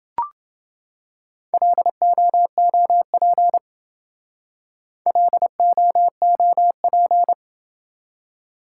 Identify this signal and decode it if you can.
Morse code, heard as a keyed tone, about 28 wpm, 700 Hz. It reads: LOOP LOOP